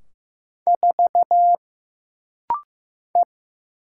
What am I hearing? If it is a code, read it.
Morse code, heard as a keyed tone, about 15 wpm, 700 Hz. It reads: 4 E
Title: Single Letter-Number Alt - Rapid-Fire 15wpm